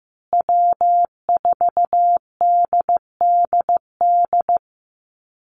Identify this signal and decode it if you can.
Morse code, heard as a keyed tone, about 15 wpm, 700 Hz. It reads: W4DDD
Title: Call Signs Repeated 2x - 15wpm